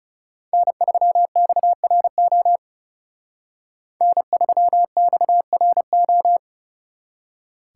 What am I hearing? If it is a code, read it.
Morse code, heard as a keyed tone, about 35 wpm, 700 Hz. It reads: N3XRO N3XRO
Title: Speed Racing - USA 1x3 Call Signs 30wpm